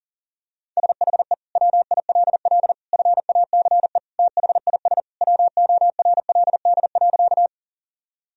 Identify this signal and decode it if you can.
Morse code, heard as a keyed tone, about 40 wpm, 700 Hz. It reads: SHE WILL FACE THIS WORLD.
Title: Sentences from Top 1000 Words - Review Part 2 - 40wpm